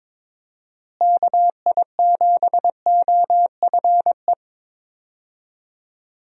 Morse code, heard as a keyed tone, about 21 wpm, 700 Hz. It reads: KI7OFE
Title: USA Call Signs - 6 Characters - Repeated 3x - 22wpm